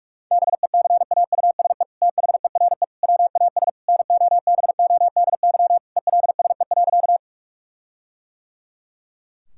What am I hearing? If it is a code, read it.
Morse code, heard as a keyed tone, about 41 wpm, 700 Hz. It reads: BECAUSE THERE WAS NOBODY ELSE.
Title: Sentences from Top 1600 Words 45wpm